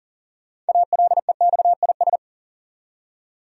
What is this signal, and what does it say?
Morse code, heard as a keyed tone, about 37 wpm, 700 Hz. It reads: ALEXIS
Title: USA Common Names - 40wpm